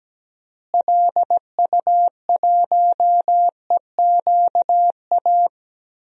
Morse code, heard as a keyed tone, about 17 wpm, 700 Hz. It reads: LU1EQA
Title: DXCC-100 Call Signs - 6 Characters - Repeated 3x - 17wpm